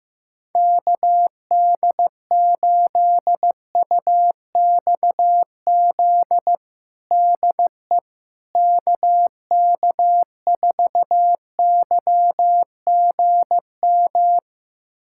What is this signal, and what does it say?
Morse code, heard as a keyed tone, about 15 wpm, 700 Hz. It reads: KD8UXZ DE KK4YGM
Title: USA Callsign DE Callsign - 6 Characters - 15wpm